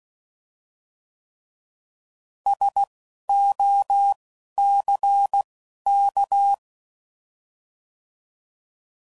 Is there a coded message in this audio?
Yes: Morse code from 2.5 to 6.5 s, heard as a keyed tone, about 16 wpm, 800 Hz. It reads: S O C K